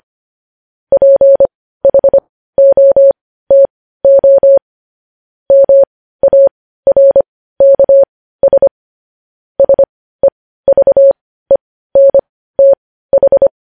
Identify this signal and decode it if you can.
Morse code, heard as a keyed tone, about 25 wpm, spaced as for 15 wpm, 550 Hz. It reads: PHOTO MARKS SEVENTH
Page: News Headlines in Morse Code at 15 WPM